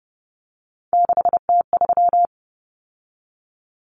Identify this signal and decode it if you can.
Morse code, heard as a keyed tone, about 30 wpm, 700 Hz. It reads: 6T3
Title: Common Cut Numbers - 3-Digit 30wpm